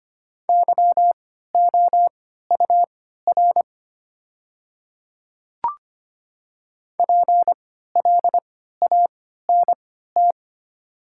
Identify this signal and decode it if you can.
Morse code, heard as a keyed tone, about 25 wpm, spaced as for 14 wpm, 690 Hz. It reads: YOUR PLANT
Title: 3x Character Spacing - Top 500 Words - 25wpm